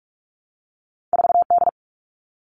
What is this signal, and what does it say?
Morse code, heard as a keyed tone, about 46 wpm, 700 Hz. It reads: 4D
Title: All Cut Numbers - 2-Digit 45wpm